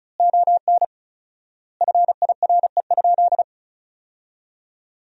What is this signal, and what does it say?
Morse code, heard as a keyed tone, about 34 wpm, 700 Hz. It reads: ON FIRE?
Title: 4x Word-Spacing - Sentences from Top 400 Words - 35wpm